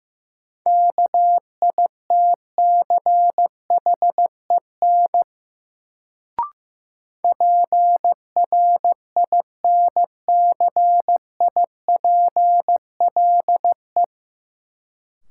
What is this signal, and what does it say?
Morse code, heard as a keyed tone, about 15 wpm, 700 Hz. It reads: KITCHEN PRINCIPLE
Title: Sentences from Top 1400 Words - New Word Drill - 15wpm